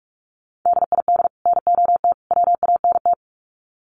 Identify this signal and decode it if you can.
Morse code, heard as a keyed tone, about 44 wpm, 700 Hz. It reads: DID NOT WANT
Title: Sets of 3 Words 45wpm